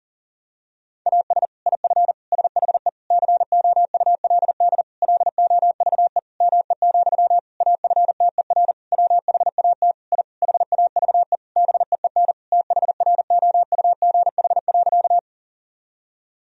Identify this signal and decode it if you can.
Morse code, heard as a keyed tone, about 40 wpm, 700 Hz. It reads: AS IF SHE COULD LOVE ME, AFTER WHAT I HAVE BEEN THROUGH.
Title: Sentences from Top 1000 Words - Review Part 1 - 40wpm